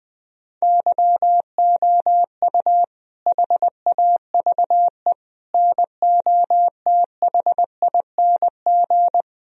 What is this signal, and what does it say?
Morse code, heard as a keyed tone, about 20 wpm, 700 Hz. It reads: YOU HAVE NOTHING
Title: Sentences from Top 1000 Words - Review Part 3 - 20wpm